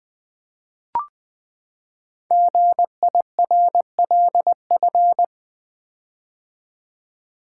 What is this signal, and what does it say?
Morse code, heard as a keyed tone, about 20 wpm, 700 Hz. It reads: GIRLF
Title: Build A Word - 5 Letters At A Time - 10-Letter Words - 20wpm